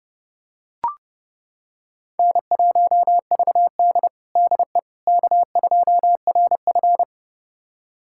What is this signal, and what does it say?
Morse code, heard as a keyed tone, about 30 wpm, 700 Hz. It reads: N1VD DE K2RF
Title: USA Callsign DE Callsign - 4 Characters - 30wpm